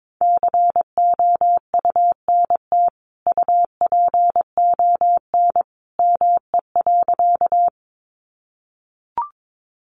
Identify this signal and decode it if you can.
Morse code, heard as a keyed tone, about 22 wpm, 700 Hz. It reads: COUNT UPON ME.
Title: Sentences from Top 1500 Words 22wpm